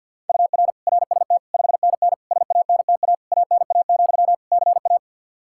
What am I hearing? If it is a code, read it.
Morse code, heard as a keyed tone, about 51 wpm, 700 Hz. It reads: UR RST 5NN SANTA ANA, CA
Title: Practice QSO - Exchange 1 - Beginning - 50wpm